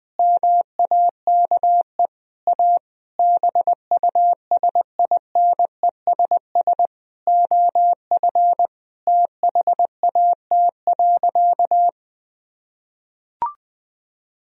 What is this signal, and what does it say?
Morse code, heard as a keyed tone, about 20 wpm, 700 Hz. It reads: MAKE A BUSINESS OF THAT.
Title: Sentences from Top 1000 Words - Review Part 4 - 20wpm